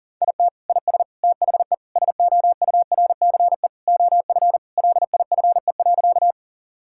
Morse code, heard as a keyed tone, about 40 wpm, 700 Hz. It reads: IT IS THE SOURCE OF LIFE.
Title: Sentences from Top 1000 Words - Review Part 5 - 40wpm